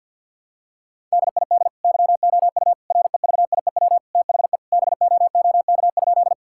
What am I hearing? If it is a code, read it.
Morse code, heard as a keyed tone, about 50 wpm, 700 Hz. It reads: DID YOU REVIEW THE BOOK?
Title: Sentences from Top 1000 Words - Review Part 6 - 50wpm